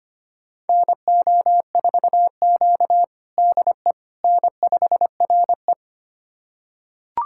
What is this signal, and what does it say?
Morse code, heard as a keyed tone, about 25 wpm, 700 Hz. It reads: NO4Q DE N5RE
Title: USA Callsign DE Callsign - 4 Characters - 25wpm